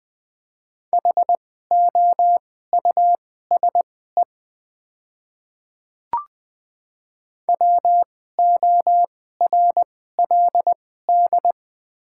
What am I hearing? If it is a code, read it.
Morse code, heard as a keyed tone, about 20 wpm, spaced as for 15 wpm, 700 Hz. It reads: HOUSE WORLD
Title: 2x Character Spacing - Top 200 Words - 20wpm